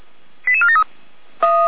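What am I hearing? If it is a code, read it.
Five-tone selective call (ZVEI), heard as quick runs of five tones, tones of about 70 ms. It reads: 88473
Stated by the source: ZVEI five-tone call 88149 received by SDR, recorded by feuerwehr-orlamuende.org.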